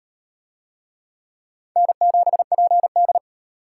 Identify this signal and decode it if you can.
Morse code, heard as a keyed tone, about 36 wpm, 700 Hz. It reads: N7PD